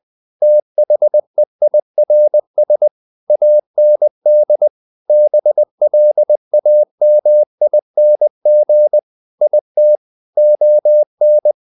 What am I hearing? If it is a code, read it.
Morse code, heard as a keyed tone, about 20 wpm, 600 Hz. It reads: THEIRS AND BLAMING IT ON
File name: CW / Practice / If20